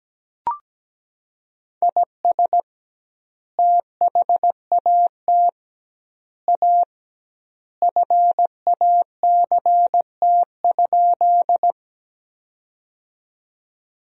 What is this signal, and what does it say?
Morse code, heard as a keyed tone, about 17 wpm, 700 Hz. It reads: IS THAT A FACT?
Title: 2x Word-Spacing - Sentences from Top 500 Words - 17wpm